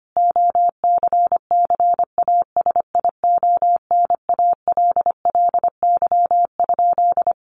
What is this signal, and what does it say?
Morse code, heard as a keyed tone, about 25 wpm, 700 Hz. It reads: OCCASIONALLY?
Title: Sentences from Top 1300 Words 25wpm